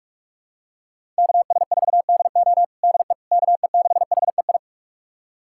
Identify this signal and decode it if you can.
Morse code, heard as a keyed tone, about 41 wpm, 700 Hz. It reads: KS4DO DE KE6HEI
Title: USA Callsign DE Callsign 45wpm